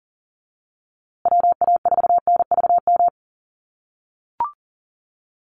Morse code, heard as a keyed tone, about 40 wpm, 700 Hz. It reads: WA4NVM
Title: USA Call Signs Repeated 2x 40wpm